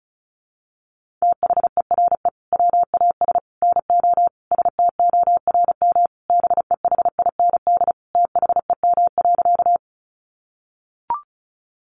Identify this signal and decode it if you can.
Morse code, heard as a keyed tone, about 34 wpm, 700 Hz. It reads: THERE WAS NO STORM BEHIND THEM.
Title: Sentences from Top 1500 Words 35wpm